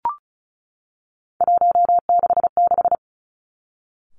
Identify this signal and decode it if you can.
Morse code, heard as a keyed tone, about 36 wpm, 700 Hz. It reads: 166
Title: Three Challenging Letter-Number Combinations 35wpm